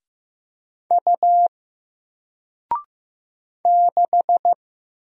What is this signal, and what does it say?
Morse code, heard as a keyed tone, about 15 wpm, 700 Hz. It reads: U 6
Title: Single Letter-Number Alt 15wpm